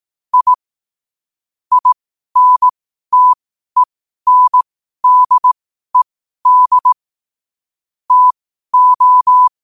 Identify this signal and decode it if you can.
Morse code, heard as a keyed tone, about 18 wpm, spaced as for 12 wpm, 1000 Hz. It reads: I INTENDED TO
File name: qotdfiles / 12wpm